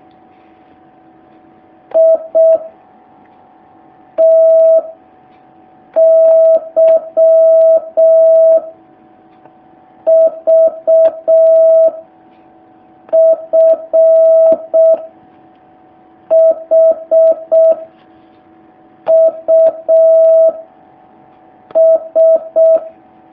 Morse code, heard as a keyed tone, about 6 wpm, 650 Hz. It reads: I T Y V F H U S
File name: jh8cbh / 2019cw